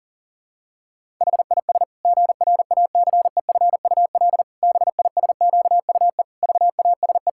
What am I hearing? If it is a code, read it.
Morse code, heard as a keyed tone, about 40 wpm, 700 Hz. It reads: HIS GRACEFUL BISQUE VASE
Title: Pangram Challenge v2 - 40wpm